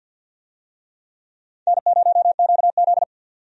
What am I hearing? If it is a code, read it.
Morse code, heard as a keyed tone, about 51 wpm, 700 Hz. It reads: N0QZ
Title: USA Call Signs - 4 Characters - 50wpm